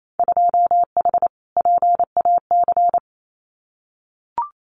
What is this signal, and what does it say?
Morse code, heard as a keyed tone, about 28 wpm, 700 Hz. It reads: 2H PAC
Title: Winter Field Day Exchanges Encore 28wpm